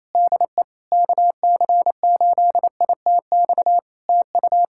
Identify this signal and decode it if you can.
Morse code, heard as a keyed tone, about 28 wpm, 700 Hz. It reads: DE KC8ITX TU
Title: Practice QSO - Exchange 4 - Ending - 28wpm